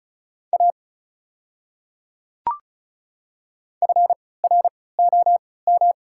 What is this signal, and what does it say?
Morse code, heard as a keyed tone, about 36 wpm, spaced as for 20 wpm, 690 Hz. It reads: A FROM
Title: 3x Character Spacing - Top 100 Words - 35wpm